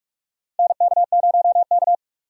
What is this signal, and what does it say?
Morse code, heard as a keyed tone, about 46 wpm, 700 Hz. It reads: NK0K
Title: USA Call Signs - 4 Characters - 45wpm